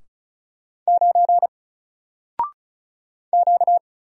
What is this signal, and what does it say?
Morse code, heard as a keyed tone, about 36 wpm, 700 Hz. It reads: 9 Q
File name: Single-Letter-Number-35wpm